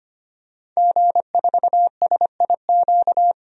Morse code, heard as a keyed tone, about 25 wpm, 700 Hz. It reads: G4SIQ
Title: DXCC-100 Call Signs - 5 Characters - 25wpm